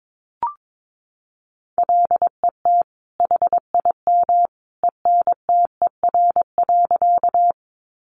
Morse code, heard as a keyed tone, about 22 wpm, 700 Hz. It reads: LET HIM ENTER.
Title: Sentences from Top 1300 Words 22wpm